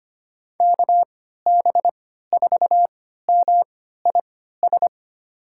Morse code, heard as a keyed tone, about 25 wpm, spaced as for 14 wpm, 700 Hz. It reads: KB4MIS
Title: USA Call Signs - 6 Characters - 3x Character Spacing - 25wpm